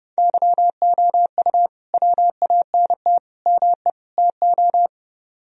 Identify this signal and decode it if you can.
Morse code, heard as a keyed tone, about 31 wpm, 700 Hz. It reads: YOU WANT ME TO